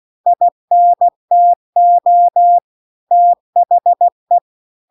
Morse code, heard as a keyed tone, about 16 wpm, 700 Hz. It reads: INTO THE